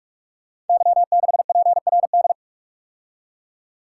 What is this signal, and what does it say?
Morse code, heard as a keyed tone, about 44 wpm, 690 Hz. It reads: Y6PRD
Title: Code Groups - 5 Letter:Numbers - 45wpm